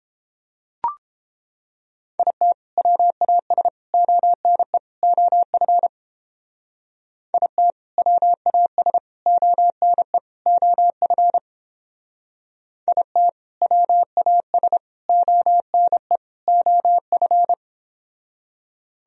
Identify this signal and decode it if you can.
Morse code, heard as a keyed tone, about 29 wpm, 700 Hz. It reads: IT WAS ONE OF IT WAS ONE OF IT WAS ONE OF